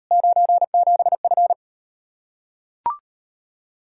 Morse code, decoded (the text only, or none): 97F